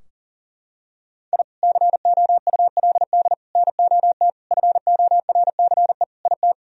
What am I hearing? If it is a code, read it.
Morse code, heard as a keyed tone, about 40 wpm, 700 Hz. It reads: I COULD NOT FORCE IT
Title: Sentences from Top 1000 Words - Review Part 3 - 40wpm